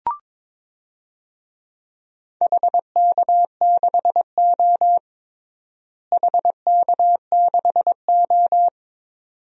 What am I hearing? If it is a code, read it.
Morse code, heard as a keyed tone, about 22 wpm, 700 Hz. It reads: HK6O HK6O
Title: DXCC-100 Call Signs - 4 Characters - Repeated 2x - 22wpm